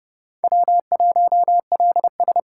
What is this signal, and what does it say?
Morse code, heard as a keyed tone, about 32 wpm, 700 Hz. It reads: W1LS
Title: USA Callsign DE Callsign - 4 Characters - 30wpm